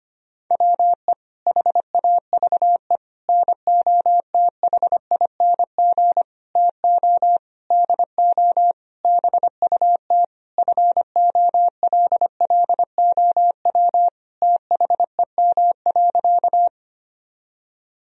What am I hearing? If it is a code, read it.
Morse code, heard as a keyed tone, about 25 wpm, 700 Hz. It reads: WE HAVE NOTHING TO DO BUT FOLLOW THEM.
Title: Sentences from Top 1000 Words - Review Part 1 - 25wpm